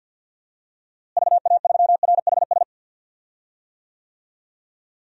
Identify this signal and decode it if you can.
Morse code, heard as a keyed tone, about 52 wpm, 700 Hz. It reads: VA3RHS
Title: DXCC-100 Call Signs - 6 Characters - Repeated 2x - 50wpm